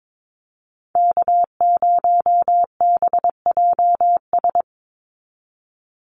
Morse code, heard as a keyed tone, about 22 wpm, 700 Hz. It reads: K0BJS
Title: USA Call Signs 22wpm